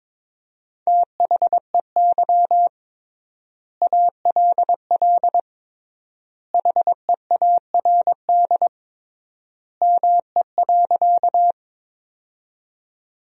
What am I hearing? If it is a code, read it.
Morse code, heard as a keyed tone, about 22 wpm, 700 Hz. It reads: THEY ALL HEARD ME.
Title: 3x Word-Spacing - Sentences from Top 400 Words - 22wpm